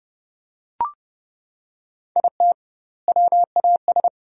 Morse code, heard as a keyed tone, about 28 wpm, 700 Hz. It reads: IT WAS
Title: 2x Word-Spacing - Sentences from Top 500 Words - 30wpm